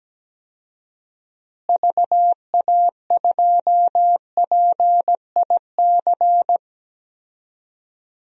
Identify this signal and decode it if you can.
Morse code, heard as a keyed tone, about 17 wpm, 700 Hz. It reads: VA2PIC
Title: DXCC-100 Call Signs - 6 Characters - Repeated 3x - 17wpm